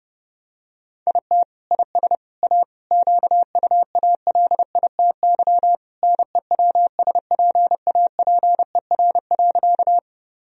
Morse code, heard as a keyed tone, about 30 wpm, 700 Hz. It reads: IT IS A QUALITY NEWSPAPER.
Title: Sentences from Top 1000 Words - Review Part 4 - 30wpm